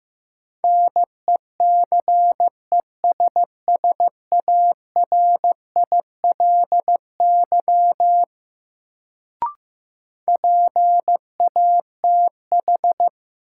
Morse code, heard as a keyed tone, about 15 wpm, 700 Hz. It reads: NECESSARILY PATH
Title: Sentences from Top 1400 Words - New Word Drill - 15wpm